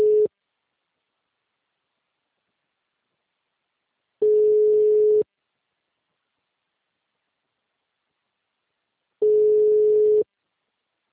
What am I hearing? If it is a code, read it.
Ringback tone (Europe, 425 Hz).
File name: out- -2001- - - .3019